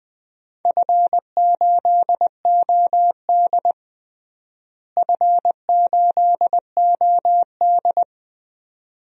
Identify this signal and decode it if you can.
Morse code, heard as a keyed tone, about 20 wpm, 700 Hz. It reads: F8OD F8OD
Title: DXCC-100 Call Signs - 4 Characters - Repeated 2x - 20wpm